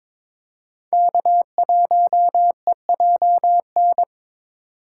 Morse code, heard as a keyed tone, about 22 wpm, 700 Hz. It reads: K1EJN
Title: USA Call Signs - 3x Character Spacing - 22wpm